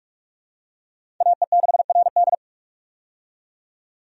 Morse code, heard as a keyed tone, about 46 wpm, 700 Hz. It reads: AE6RD
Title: USA Call Signs - 5 Characters - 45wpm